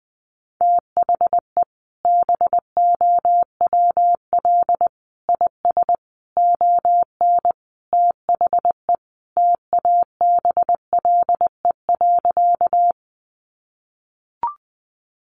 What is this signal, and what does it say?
Morse code, heard as a keyed tone, about 19 wpm, 700 Hz. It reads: THE BOWL IS ON THE TABLE.
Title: Sentences from Top 1200 Words 20wpm